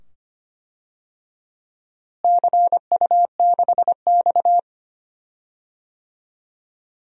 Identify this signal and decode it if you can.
Morse code, heard as a keyed tone, about 25 wpm, 700 Hz. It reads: CU6X
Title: DXCC-100 Call Signs - 4 Characters - Repeated 2x - 25wpm